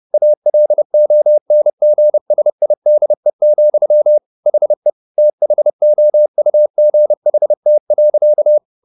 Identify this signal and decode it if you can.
Morse code, heard as a keyed tone, about 30 wpm, 600 Hz. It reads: ALONGSIDE, HE THOUGHT.